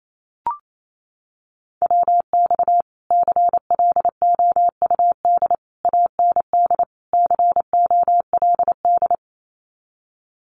Morse code, heard as a keyed tone, about 28 wpm, 700 Hz. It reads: WX CLOUD AND COLD
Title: Common QSO Phrases Encore 28wpm